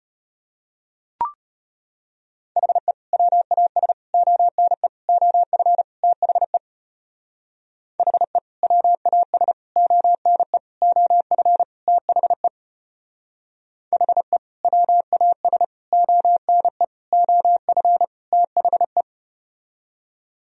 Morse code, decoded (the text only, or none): HE WAS ONE OF THE HE WAS ONE OF THE HE WAS ONE OF THE